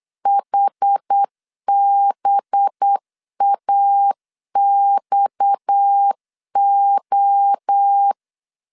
Morse code, heard as a keyed tone, about 8 wpm, 800 Hz. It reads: HBAXO
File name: CW / DET / 1 6 1500C S B 8wpm 5142